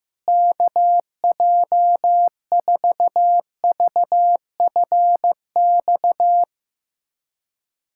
Morse code, heard as a keyed tone, about 14 wpm, 700 Hz. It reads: KJ4VFX
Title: USA Callsign DE Callsign - 6 Characters - 15wpm